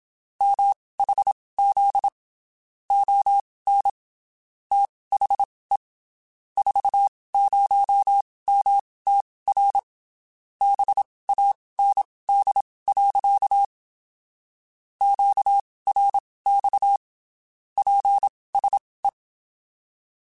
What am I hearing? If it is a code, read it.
Morse code, heard as a keyed tone, about 27 wpm, spaced as for 19 wpm, 800 Hz. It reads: MHZ ON THE 40MTR BAND. QRX PSE
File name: (spaced) / QSOs Spaced-mp3s / kc7sac @25 sp